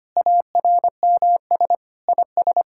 Morse code, heard as a keyed tone, about 25 wpm, 700 Hz. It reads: ARMS IS
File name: practice / 25wpm / Sentences with 9-Word Phrases - 25wpm